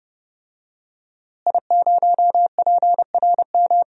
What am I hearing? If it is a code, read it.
Morse code, heard as a keyed tone, about 30 wpm, 700 Hz. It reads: I0PRM
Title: DXCC-100 Call Signs - 5 Characters - Repeated 3x - 30wpm